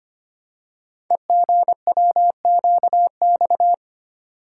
Morse code, heard as a keyed tone, about 25 wpm, 700 Hz. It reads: EGWQX
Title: Code Groups - 5 Letters - 25wpm